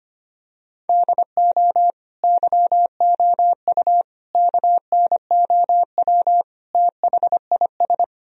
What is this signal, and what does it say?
Morse code, heard as a keyed tone, about 25 wpm, 700 Hz. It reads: DO YOU KNOW THIS
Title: Sentences from Top 1000 Words - Review Part 2 - 25wpm